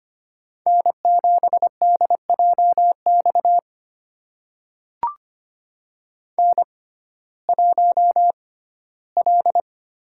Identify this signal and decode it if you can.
Morse code, heard as a keyed tone, about 25 wpm, spaced as for 15 wpm, 700 Hz. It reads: N7DJX N1L
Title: USA Call Signs - 5 Characters - 6x Character Spacing - 25wpm